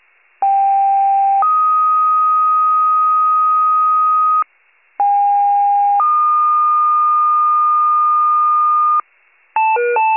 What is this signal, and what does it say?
Two-tone page, heard as a one-second tone, then a three-second tone, about 790 then 1250 Hz, about 810 then 1220 Hz. It is heard from 0.4 to 9.0 s.
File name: pager / 73 RESCUE / wcalert